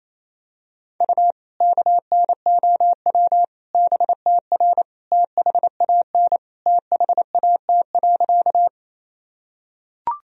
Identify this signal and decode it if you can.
Morse code, heard as a keyed tone, about 28 wpm, 700 Hz. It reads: U KNOW BTR THAN THAT.
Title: Sentences from Top 400 Words and Abbreviations 28wpm